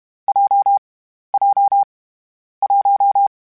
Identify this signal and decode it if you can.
Morse code, heard as a keyed tone, about 32 wpm, 800 Hz. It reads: J J 1
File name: Difficult Combos / Difficult Combos-mp3s / WPJ WPJ1 @30